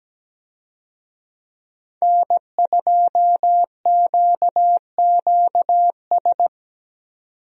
Morse code, heard as a keyed tone, about 17 wpm, 700 Hz. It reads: N2QQS
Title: USA Call Signs - 5 Characters - 17wpm